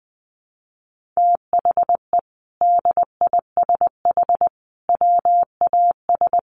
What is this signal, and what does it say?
Morse code, heard as a keyed tone, about 20 wpm, 700 Hz. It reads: THE DISH WAS
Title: Sentences from Top 1400 Words 20wpm